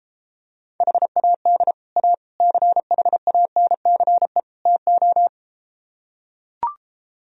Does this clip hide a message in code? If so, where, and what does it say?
Yes: Morse code from 0.8 to 5.3 s, heard as a keyed tone, about 33 wpm, 700 Hz. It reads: HAD A CHANCE TO